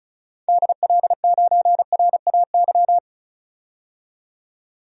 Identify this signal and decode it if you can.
Morse code, heard as a keyed tone, about 34 wpm, 700 Hz. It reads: DL9RAY
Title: DXCC-100 Call Signs - 6 Characters - 35wpm